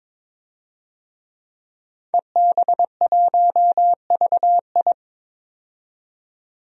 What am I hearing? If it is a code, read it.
Morse code, heard as a keyed tone, about 21 wpm, 700 Hz. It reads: EB1VI
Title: DXCC-100 Call Signs - 5 Characters - 22wpm